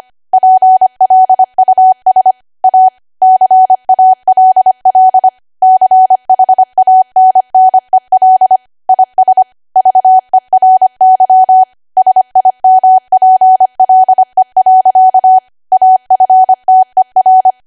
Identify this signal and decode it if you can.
Morse code, heard as a keyed tone, about 25 wpm, 750 Hz. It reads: PLUS A CALL CHANNEL IS VERY SIMPLE. AFTER
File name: Archive / 25 WPM / 25WPM